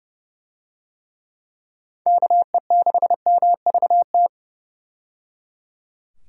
Morse code, heard as a keyed tone, about 30 wpm, 700 Hz. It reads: KE6MVT
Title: USA Call Signs - 6 Characters - 30wpm